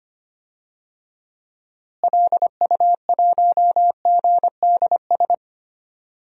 Morse code, heard as a keyed tone, about 25 wpm, 700 Hz. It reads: LU1GDS
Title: DXCC-100 Call Signs - 6 Characters - Repeated 3x - 25wpm